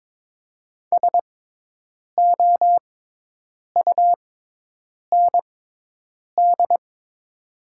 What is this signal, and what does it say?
Morse code, heard as a keyed tone, about 22 wpm, spaced as for 8 wpm, 700 Hz. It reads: SOUND
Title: 6x Character Spacing - Top 200 Words - 22wpm